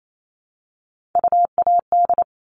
Morse code, heard as a keyed tone, about 28 wpm, 700 Hz. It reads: UAD
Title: All Cut Numbers - 3-Digit 28wpm